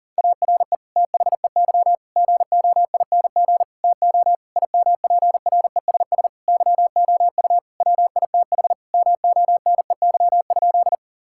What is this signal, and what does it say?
Morse code, heard as a keyed tone, about 40 wpm, 700 Hz. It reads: ARE THEY GOING TO IMPRESS YOU WITH MONEY?
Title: Sentences from Top 1600 Words 40wpm